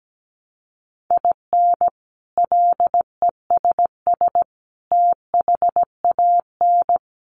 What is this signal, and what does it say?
Morse code, heard as a keyed tone, about 17 wpm, 700 Hz. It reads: IN LESS THAN
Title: Sentences from Top 1300 Words 17wpm